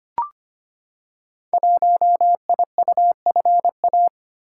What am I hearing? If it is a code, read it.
Morse code, heard as a keyed tone, about 25 wpm, 700 Hz. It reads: 1IUFA
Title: Code Groups - 5 Letter:Numbers - 25wpm